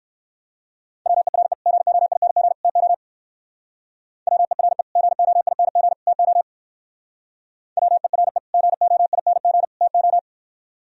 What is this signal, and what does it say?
Morse code, heard as a keyed tone, about 58 wpm, 700 Hz. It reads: WERE GOING TO WERE GOING TO WERE GOING TO